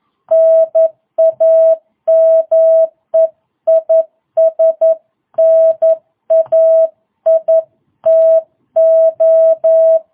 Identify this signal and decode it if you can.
Morse code, heard as a keyed tone, about 11 wpm, 660 Hz. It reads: NAME IS NAITO